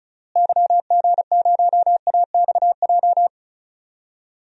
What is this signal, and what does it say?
Morse code, heard as a keyed tone, about 36 wpm, 700 Hz. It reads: YG0AXJ
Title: DXCC-100 Call Signs - 6 Characters - Repeated 3x - 35wpm